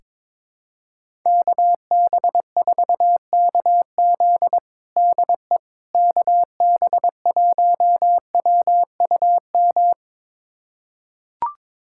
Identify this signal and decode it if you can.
Morse code, heard as a keyed tone, about 22 wpm, 700 Hz. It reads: KB4KZ DE KB1WUM
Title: USA Callsign DE Callsign 22wpm